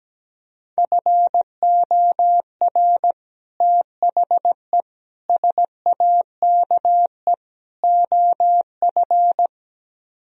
Morse code, heard as a keyed tone, about 17 wpm, 700 Hz. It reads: FOR THE SAKE OF
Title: Speed-Racing - Sets of 4 Words 17wpm